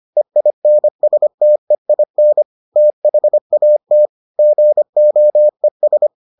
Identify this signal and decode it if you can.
Morse code, heard as a keyed tone, about 25 wpm, 600 Hz. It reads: EINSTEIN THAT GOES